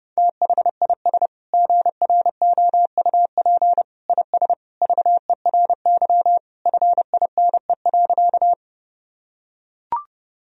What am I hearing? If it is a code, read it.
Morse code, heard as a keyed tone, about 32 wpm, 700 Hz. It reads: THIS GROUP IS VERY FINE.
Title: Sentences from Top 1000 Words - Review Part 1 - 30wpm